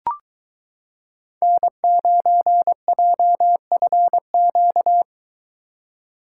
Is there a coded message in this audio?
Yes: Morse code from 1.4 to 5.0 s, heard as a keyed tone, about 22 wpm, 700 Hz. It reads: N9JFQ